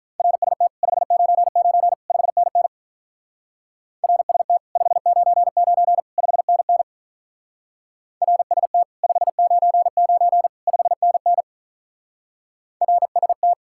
Morse code, heard as a keyed tone, about 43 wpm, 700 Hz. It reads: RST 599 5NN RST 599 5NN RST 599 5NN RST
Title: Speed-Racing Common QSO Phrases Encore - 35wpm